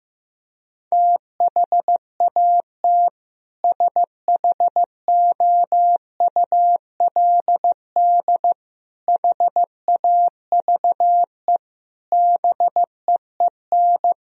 Morse code, heard as a keyed tone, about 16 wpm, 700 Hz. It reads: THAT SHOULD HAVE BEEN
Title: Sentences from Top 1000 Words - Review Part 2 - 15wpm